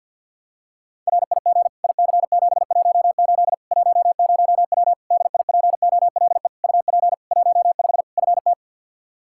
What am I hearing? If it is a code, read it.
Morse code, heard as a keyed tone, about 51 wpm, 700 Hz. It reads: RIG IC718 10W DIPOLE UP 15 FT
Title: Practice QSO - Exchange 2 - Info - 50wpm